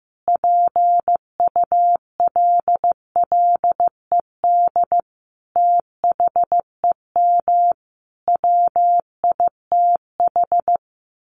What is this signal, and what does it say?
Morse code, heard as a keyed tone, about 15 wpm, 700 Hz. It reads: PULLED THEM WITH
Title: A Sound Of Thunder 15wpm